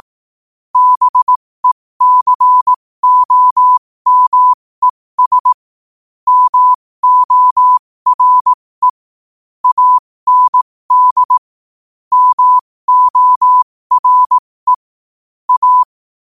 Morse code, heard as a keyed tone, about 18 wpm, spaced as for 15 wpm, 1000 Hz. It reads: BECOMES MORE AND MORE A